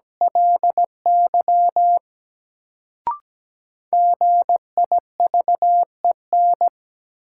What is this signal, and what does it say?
Morse code, heard as a keyed tone, about 17 wpm, 700 Hz. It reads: LY GIVEN
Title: Sentences from Top 1400 Words - New Word Drill Alt - 17wpm